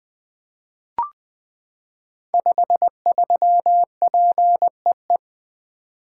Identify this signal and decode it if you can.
Morse code, heard as a keyed tone, about 20 wpm, 700 Hz. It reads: 53PEE
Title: Code Groups - 5 Letter:Numbers - 20wpm